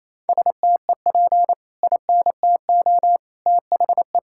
Morse code, heard as a keyed tone, about 28 wpm, 700 Hz. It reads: STEP INTO THE
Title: Sentences from Top 1600 Words 28wpm